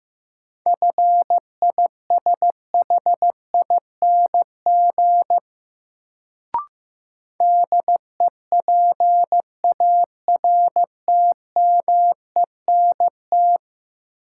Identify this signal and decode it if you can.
Morse code, heard as a keyed tone, about 15 wpm, 700 Hz. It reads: FISHING DEPARTMENT
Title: Sentences from Top 1000 Words - Review Part 5 - New Word Drill - 15wpm